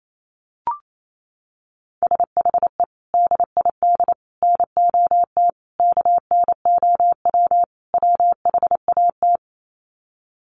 Morse code, heard as a keyed tone, about 28 wpm, 700 Hz. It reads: SHE DID NOT KNOW WHAT
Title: Sets of 5 Words Encore2 28wpm